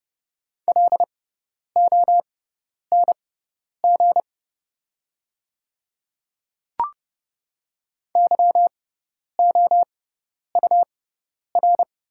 Morse code, heard as a keyed tone, about 29 wpm, spaced as for 10 wpm, 700 Hz. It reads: LONG YOUR
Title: 6x Character Spacing - Top 100 Words - 30wpm